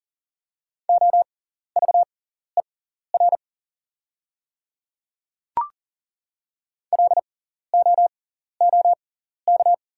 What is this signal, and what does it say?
Morse code, heard as a keyed tone, about 38 wpm, spaced as for 14 wpm, 700 Hz. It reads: OVER LOOK
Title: 6x Character Spacing - Top 200 Words - 40wpm